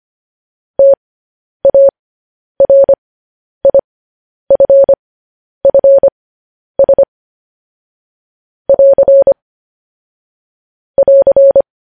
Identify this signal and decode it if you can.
Morse code, heard as a keyed tone, about 25 wpm, spaced as for 10 wpm, 550 Hz. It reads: TARIFFS <AR> <AR>
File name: podcast / News headlines-1x-25x10